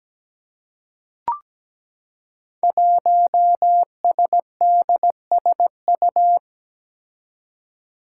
Morse code, heard as a keyed tone, about 17 wpm, 700 Hz. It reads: 1SDSU
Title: Five Challenging Letter-Number Combinations 17wpm